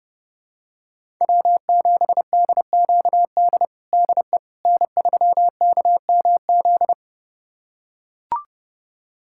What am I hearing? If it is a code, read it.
Morse code, heard as a keyed tone, about 30 wpm, 700 Hz. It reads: W7DQD DE N3KMZ
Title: USA Callsign DE Callsign - 5 Characters - 30wpm